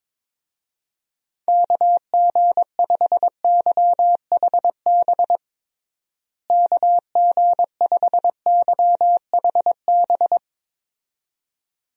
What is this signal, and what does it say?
Morse code, heard as a keyed tone, about 21 wpm, 700 Hz. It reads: KG5YHB KG5YHB
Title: USA Call Signs - 6 Characters - Repeated 2x - 22wpm